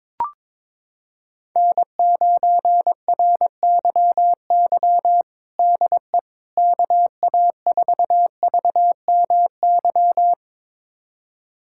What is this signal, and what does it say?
Morse code, heard as a keyed tone, about 22 wpm, 700 Hz. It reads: N9RYY DE KA4VMY
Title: USA Callsign DE Callsign 22wpm